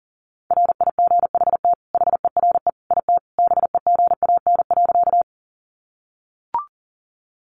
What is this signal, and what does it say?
Morse code, heard as a keyed tone, about 40 wpm, 700 Hz. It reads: RIGHT HERE IT BEGAN.
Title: Sentences from Top 300 Words 40wpm